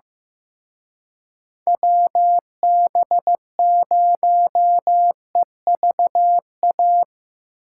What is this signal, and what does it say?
Morse code, heard as a keyed tone, about 17 wpm, 700 Hz. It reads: WB0EVA
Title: Speed Racing - USA 2x3 Call Signs 15wpm